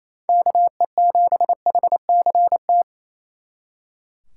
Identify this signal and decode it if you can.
Morse code, heard as a keyed tone, about 33 wpm, 700 Hz. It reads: KE7HCT
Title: Speed Racing - USA 2x3 Call Signs 28wpm